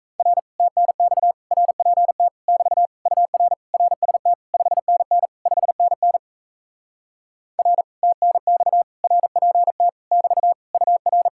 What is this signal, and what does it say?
Morse code, heard as a keyed tone, about 38 wpm, 700 Hz. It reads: R TNX RPT <BT> UR RST 5NN 5NN R TNX RPT <BT> UR